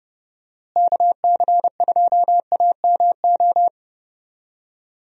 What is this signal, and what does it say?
Morse code, heard as a keyed tone, about 31 wpm, 700 Hz. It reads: KC2AMO
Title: USA Call Signs - 6 Characters - Repeated 2x - 30wpm